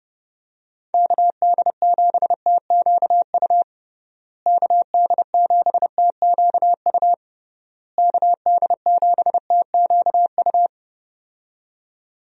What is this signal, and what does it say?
Morse code, heard as a keyed tone, about 30 wpm, 700 Hz. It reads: KD7TQU KD7TQU KD7TQU
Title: USA Call Signs - 6 Characters - Repeated 3x - 30wpm